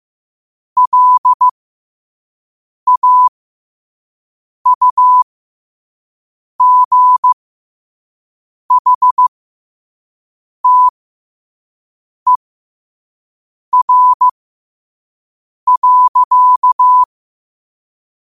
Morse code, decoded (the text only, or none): LAUGHTER.